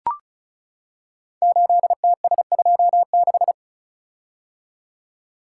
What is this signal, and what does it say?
Morse code, heard as a keyed tone, about 36 wpm, 700 Hz. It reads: 8TS26
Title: Code Groups - 5 Letter:Numbers - 35wpm